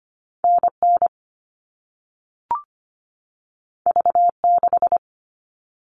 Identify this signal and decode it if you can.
Morse code, heard as a keyed tone, about 25 wpm, 700 Hz. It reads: N N V 6
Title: All Cut Numbers - 2-Digit 25wpm